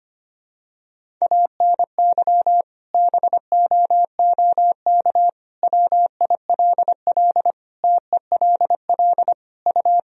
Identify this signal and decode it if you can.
Morse code, heard as a keyed tone, about 25 wpm, 700 Hz. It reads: ANY BOOK WILL TELL U
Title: Sentences from Top 300 Words and Abbreviations 25wpm